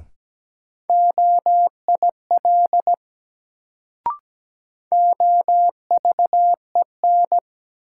Morse code, heard as a keyed tone, about 17 wpm, 700 Hz. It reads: OIL OVEN
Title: Sentences from Top 1000 Words - Review Part 4 - New Word Drill - 17wpm